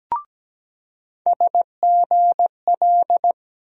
Morse code, heard as a keyed tone, about 17 wpm, 700 Hz. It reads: SGL
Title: USA Call Sign Postfixes - 3 Letters 17wpm